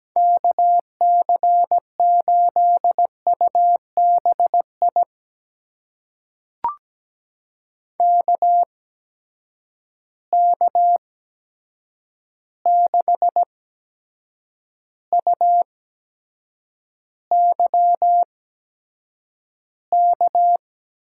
Morse code, heard as a keyed tone, about 17 wpm, spaced as for 7 wpm, 700 Hz. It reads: KC8UBI KK6UYK
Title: USA Call Signs - 6 Characters - 8x Character Spacing - 17wpm